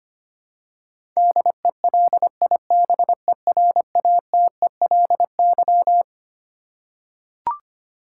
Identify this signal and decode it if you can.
Morse code, heard as a keyed tone, about 25 wpm, 700 Hz. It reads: DELIBERATELY
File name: Twelve Letter Words - 25wpm